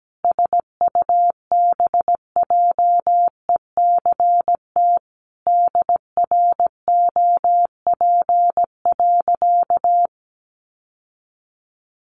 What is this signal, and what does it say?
Morse code, heard as a keyed tone, about 17 wpm, 700 Hz. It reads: SUBJECT DROP.
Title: Sentences from Top 1400 Words 17wpm